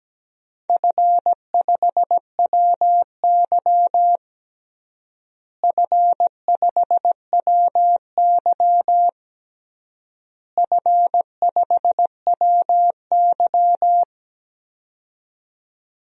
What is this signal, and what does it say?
Morse code, heard as a keyed tone, about 17 wpm, 700 Hz. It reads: F5WY F5WY F5WY
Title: DXCC-100 Call Signs - 4 Characters - Repeated 3x - 17wpm